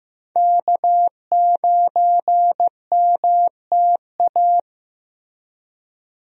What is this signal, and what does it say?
Morse code, heard as a keyed tone, about 15 wpm, 700 Hz. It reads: K9MTA
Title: USA Call Signs - 5 Characters - Repeated 2x - 15wpm